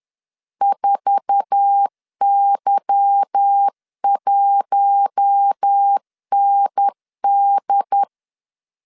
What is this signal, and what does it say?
Morse code, heard as a keyed tone, about 10 wpm, 800 Hz. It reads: 4Y1ND
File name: LEZIONI / 8 YJ45 / 8 YJ4 10wpm 2633